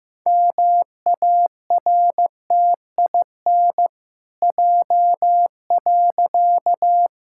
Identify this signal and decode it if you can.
Morse code, heard as a keyed tone, about 15 wpm, 700 Hz. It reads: MARTIN J.
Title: Pangram Challenge v2 - 15wpm